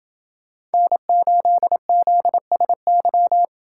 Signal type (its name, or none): Morse code